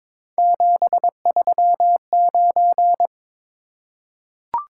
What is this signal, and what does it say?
Morse code, heard as a keyed tone, about 22 wpm, 700 Hz. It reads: 739
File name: practice / 22wpm / Three Letter-Number Combinations - 22wpm